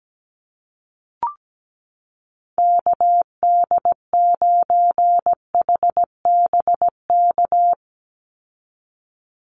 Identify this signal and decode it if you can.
Morse code, heard as a keyed tone, about 17 wpm, 700 Hz. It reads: KD9HBK
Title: USA Call Signs 17wpm